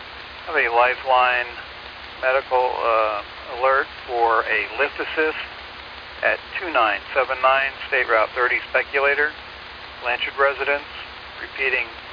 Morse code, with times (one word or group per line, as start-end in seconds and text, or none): none